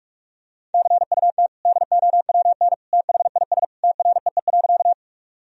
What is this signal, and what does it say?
Morse code, heard as a keyed tone, about 46 wpm, 700 Hz. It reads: CUT DOWN THIS TREE.
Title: Sentences from Top 1000 Words - Review Part 1 - 45wpm